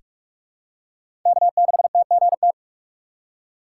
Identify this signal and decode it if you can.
Morse code, heard as a keyed tone, about 46 wpm, 700 Hz. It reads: K6TGT
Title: USA Call Signs - 3x Character Spacing - 45wpm